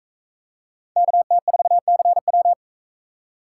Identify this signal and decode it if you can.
Morse code, heard as a keyed tone, about 38 wpm, 700 Hz. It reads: KT4CW